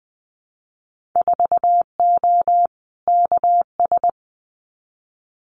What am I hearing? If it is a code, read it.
Morse code, heard as a keyed tone, about 20 wpm, 700 Hz. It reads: 4O KS
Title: Winter Field Day Exchanges Encore 20wpm